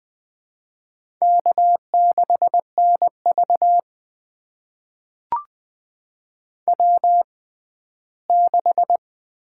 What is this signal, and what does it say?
Morse code, heard as a keyed tone, about 20 wpm, spaced as for 14 wpm, 700 Hz. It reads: K6NV W6
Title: USA Call Signs - 4 Characters - 6x Character Spacing - 20wpm